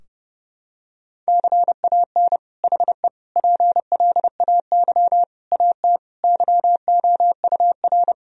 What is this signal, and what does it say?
Morse code, heard as a keyed tone, about 30 wpm, 700 Hz. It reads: CAN HE PLAY AT YOUR
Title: Sentences from Top 1000 Words - Review Part 4 - 30wpm